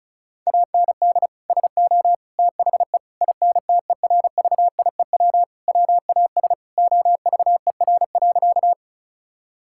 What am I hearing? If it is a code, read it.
Morse code, heard as a keyed tone, about 36 wpm, 700 Hz. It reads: AND SO THE INTERVIEW WAS OVER.
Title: Sentences from Top 1600 Words 35wpm